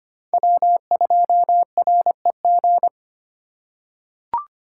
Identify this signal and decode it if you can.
Morse code, heard as a keyed tone, about 25 wpm, 700 Hz. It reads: W2REG
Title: USA Call Signs - 5 Characters - 3x Character Spacing - 25wpm